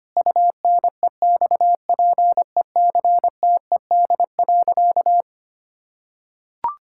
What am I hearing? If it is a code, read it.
Morse code, heard as a keyed tone, about 25 wpm, 700 Hz. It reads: UNEXPECTED.
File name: practice / 25wpm / Sentences with 2-Word Phrases - 25wpm